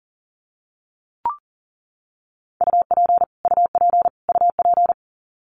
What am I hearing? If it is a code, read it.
Morse code, heard as a keyed tone, about 40 wpm, 700 Hz. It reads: UP UP UP
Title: Top 200 Words Repeated 40wpm